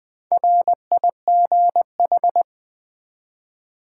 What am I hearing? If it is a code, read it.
Morse code, heard as a keyed tone, about 20 wpm, 700 Hz. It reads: RIGH
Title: Build A Word - 1 Letter At A Time - 5-Letter Words - 20wpm